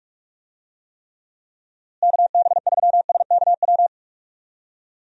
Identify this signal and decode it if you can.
Morse code, heard as a keyed tone, about 46 wpm, 700 Hz. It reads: KB3SKW
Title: USA Call Signs - 6 Characters - Repeated 3x - 45wpm